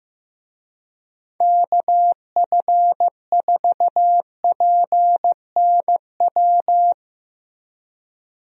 Morse code, heard as a keyed tone, about 15 wpm, 700 Hz. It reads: KF4PNW
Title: USA Call Signs - 6 Characters - Repeated 2x - 15wpm